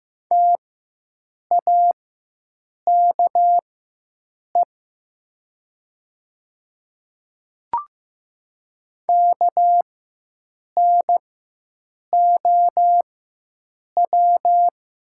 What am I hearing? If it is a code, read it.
Morse code, heard as a keyed tone, about 15 wpm, spaced as for 7 wpm, 700 Hz. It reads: TAKE KNOW
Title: 4x Character Spacing - Top 300 Words - 15wpm